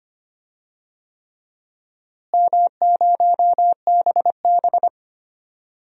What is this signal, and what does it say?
Morse code, heard as a keyed tone, about 25 wpm, 700 Hz. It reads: M0BB
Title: DXCC-100 Call Signs - 4 Characters - Repeated 3x - 25wpm